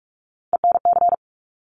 Morse code, heard as a keyed tone, about 46 wpm, 700 Hz. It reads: ENC
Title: Top Three Letter Combinations in Words 45wpm